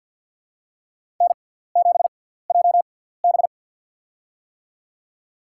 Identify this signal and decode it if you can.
Morse code, heard as a keyed tone, about 52 wpm, spaced as for 17 wpm, 700 Hz. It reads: N7JB
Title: USA Call Signs - 4 Characters - 6x Character Spacing - 50wpm